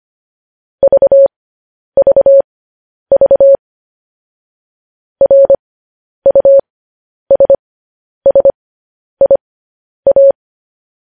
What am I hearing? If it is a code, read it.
Morse code, heard as a keyed tone, about 25 wpm, spaced as for 10 wpm, 550 Hz. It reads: VVV RUSSIA
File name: podcast / News headlines-1x-25x10